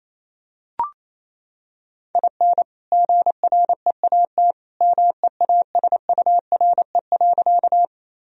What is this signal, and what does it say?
Morse code, heard as a keyed tone, about 27 wpm, 700 Hz. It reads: IN GREAT MEASURE.
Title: Sentences from Top 1000 Words - Review Part 2 - 28wpm